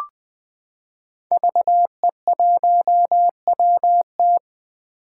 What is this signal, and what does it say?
Morse code, heard as a keyed tone, about 20 wpm, 700 Hz. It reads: VE1WT
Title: 2023 SOTA Chaser Call Signs - 20wpm